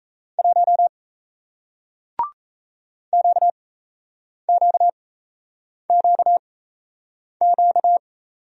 Morse code, heard as a keyed tone, about 38 wpm, spaced as for 8 wpm, 700 Hz. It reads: 1 QQQQ